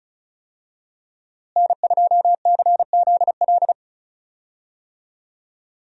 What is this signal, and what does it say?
Morse code, heard as a keyed tone, about 34 wpm, 700 Hz. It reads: N2CZL
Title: USA Call Signs - 5 Characters - Repeated 2x - 35wpm